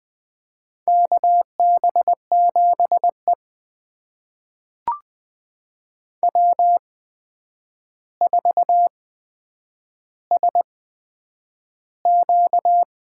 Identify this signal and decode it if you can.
Morse code, heard as a keyed tone, about 20 wpm, spaced as for 9 wpm, 700 Hz. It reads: KB7E W4SQ
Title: USA Call Signs - 4 Characters - 8x Character Spacing - 20wpm